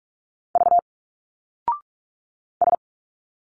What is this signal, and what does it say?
Morse code, heard as a keyed tone, about 46 wpm, 700 Hz. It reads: V S
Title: Single Letters 45wpm